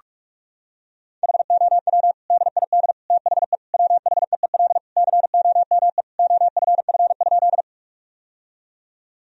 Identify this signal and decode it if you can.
Morse code, heard as a keyed tone, about 46 wpm, 700 Hz. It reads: HOW DID THE WHEEL COME OFF?
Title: Sentences from Top 1000 Words - Review Part 3 - 45wpm